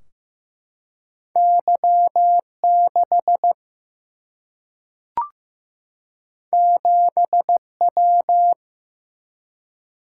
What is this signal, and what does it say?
Morse code, heard as a keyed tone, about 15 wpm, 700 Hz. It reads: Y6 7W
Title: Single-Letter Single-Number 15wpm